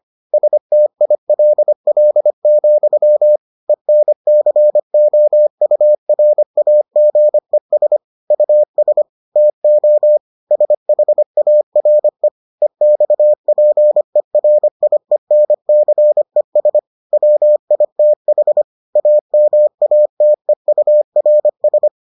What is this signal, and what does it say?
Morse code, heard as a keyed tone, about 25 wpm, 600 Hz. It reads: STILL, ENCOURAGES US TO SHARE EXPERIENCES WITH AMATEURS